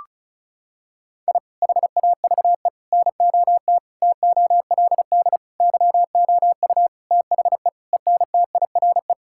Morse code, heard as a keyed tone, about 34 wpm, 700 Hz. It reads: I HAVE NOT TOLD YOU THE ENTIRE
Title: Sentences from Top 1000 Words - Review Part 5 - 35wpm